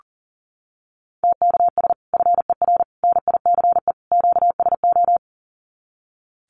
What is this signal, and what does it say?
Morse code, heard as a keyed tone, about 40 wpm, 700 Hz. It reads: TKS FER NICE QSO
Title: Common QSO Phrases 40wpm